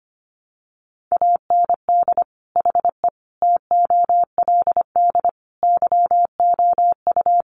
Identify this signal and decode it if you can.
Morse code, heard as a keyed tone, about 25 wpm, 700 Hz. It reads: AND HE TOLD YOU
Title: Sentences from Top 500 Words 25wpm